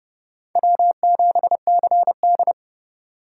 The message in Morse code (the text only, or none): W7CD